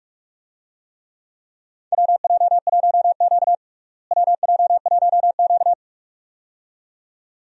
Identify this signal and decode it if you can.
Morse code, heard as a keyed tone, about 43 wpm, 700 Hz. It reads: WJ1Q WJ1Q
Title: USA Call Signs - 4 Characters - Repeated 2x - 45wpm